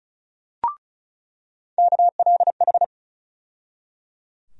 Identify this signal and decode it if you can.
Morse code, heard as a keyed tone, about 34 wpm, 700 Hz. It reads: KLH
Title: USA Call Sign Postfixes - 3 Letters 35wpm